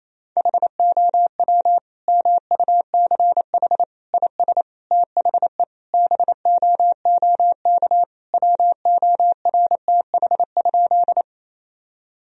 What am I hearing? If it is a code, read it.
Morse code, heard as a keyed tone, about 28 wpm, 700 Hz. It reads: HOW MUCH IS THE BOOK WORTH?
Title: Sentences from Top 1000 Words - Review Part 1 - 28wpm